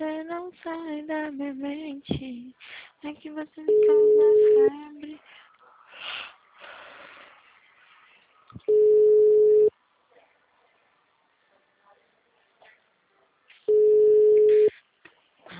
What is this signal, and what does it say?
Ringback tone (Europe, 425 Hz), with speech.